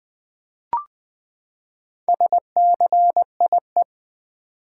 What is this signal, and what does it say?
Morse code, heard as a keyed tone, about 20 wpm, 700 Hz. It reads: SCIE
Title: Build A Word - 4 Letters At A Time - 10-Letter Words - 20wpm